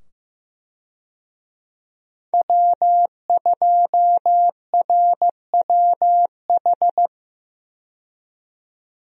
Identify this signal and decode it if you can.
Morse code, heard as a keyed tone, about 16 wpm, 700 Hz. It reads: W2RWH
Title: USA Call Signs - 5 Characters - 15wpm